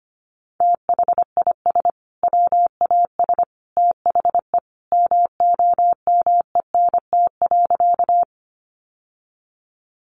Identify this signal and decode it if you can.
Morse code, heard as a keyed tone, about 25 wpm, 700 Hz. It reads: THIS WAS THE MOMENT.
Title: Sentences from Top 1000 Words 25wpm